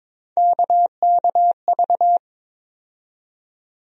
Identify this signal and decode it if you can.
Morse code, heard as a keyed tone, about 22 wpm, 700 Hz. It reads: KKV
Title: USA Call Sign Postfixes - 3 Letters 22wpm